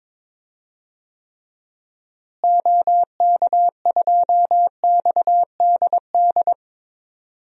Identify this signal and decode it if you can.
Morse code, heard as a keyed tone, about 22 wpm, 700 Hz. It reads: OK2XDD
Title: DXCC-100 Call Signs - 6 Characters - Repeated 2x - 22wpm